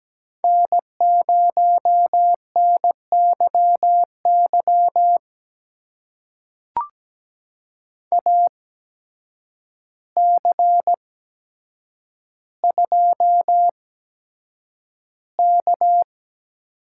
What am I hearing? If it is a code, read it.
Morse code, heard as a keyed tone, about 17 wpm, spaced as for 8 wpm, 700 Hz. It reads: N0NYY AC2K
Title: USA Call Signs - 5 Characters - 8x Character Spacing - 17wpm